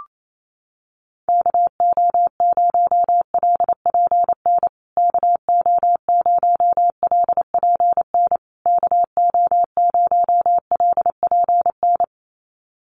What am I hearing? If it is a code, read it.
Morse code, heard as a keyed tone, about 28 wpm, 690 Hz. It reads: KO0LPN KO0LPN KO0LPN
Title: USA Call Signs Repeated 3x 28wpm